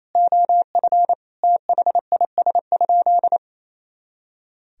Morse code, heard as a keyed tone, about 28 wpm, 700 Hz. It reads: OF THIS?
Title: Sentences from Top 200 Words and Abbreviations 28wpm